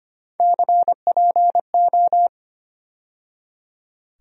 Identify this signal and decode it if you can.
Morse code, heard as a keyed tone, about 25 wpm, 700 Hz. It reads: CPO
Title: USA Call Sign Postfixes - 3 Letters 25wpm